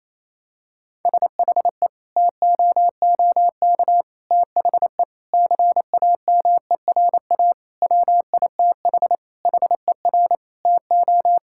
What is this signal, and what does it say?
Morse code, heard as a keyed tone, about 28 wpm, 700 Hz. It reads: SHE TOOK THE CAMERA WITH HER TO
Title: Sentences from Top 1000 Words - Review Part 5 - 28wpm